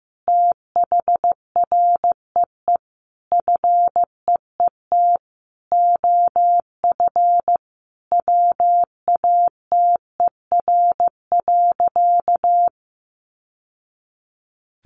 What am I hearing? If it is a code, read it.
Morse code, heard as a keyed tone, about 15 wpm, 700 Hz. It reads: THREE FEET OF WATER.
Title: Sentences from Top 700 Words 15wpm